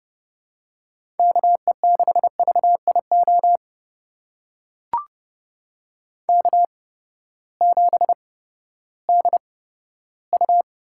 Morse code, heard as a keyed tone, about 31 wpm, spaced as for 15 wpm, 700 Hz. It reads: KE6VIO K7DU
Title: USA Call Signs - 8x Character Spacing - 30wpm